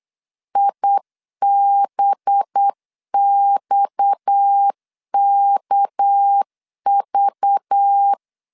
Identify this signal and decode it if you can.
Morse code, heard as a keyed tone, about 8 wpm, 800 Hz. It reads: IBXKV
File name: 8wpm